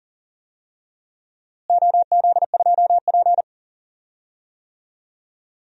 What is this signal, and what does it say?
Morse code, heard as a keyed tone, about 40 wpm, 700 Hz. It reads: OZ2P
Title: DXCC-100 Call Signs - 4 Characters - Repeated 3x - 40wpm